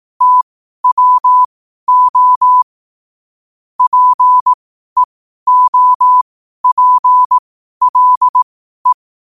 Morse code, heard as a keyed tone, about 18 wpm, spaced as for 12 wpm, 1000 Hz. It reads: TWO PEOPLE
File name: qotdfiles / 12wpm